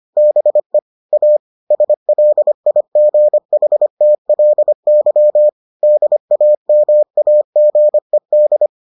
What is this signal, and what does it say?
Morse code, heard as a keyed tone, about 25 wpm, 600 Hz. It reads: BE A SLIGHTLY DAMAGED